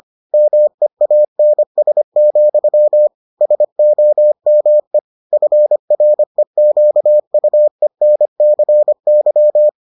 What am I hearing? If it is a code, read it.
Morse code, heard as a keyed tone, about 25 wpm, 600 Hz. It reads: MEANS, SOME FREQUENCY